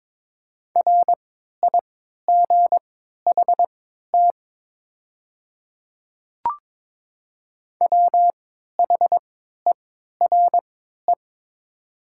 Morse code, heard as a keyed tone, about 22 wpm, spaced as for 12 wpm, 700 Hz. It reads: RIGHT WHERE
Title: 3x Character Spacing - Top 200 Words - 22wpm